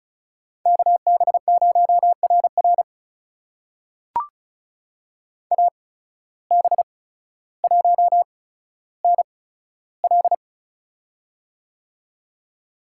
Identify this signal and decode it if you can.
Morse code, heard as a keyed tone, about 34 wpm, spaced as for 15 wpm, 700 Hz. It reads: KB0RR AB1NL
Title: USA Call Signs - 5 Characters - 8x Character Spacing - 35wpm